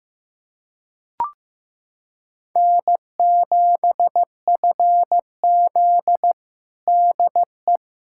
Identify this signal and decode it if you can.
Morse code, heard as a keyed tone, about 16 wpm, 700 Hz. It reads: N7FZ DE
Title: USA Callsign DE Callsign - 4 Characters - 15wpm